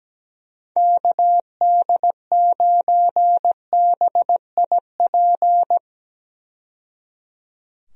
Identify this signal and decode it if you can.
Morse code, heard as a keyed tone, about 17 wpm, 700 Hz. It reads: KD9BIP
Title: USA Call Signs - 6 Characters - 17wpm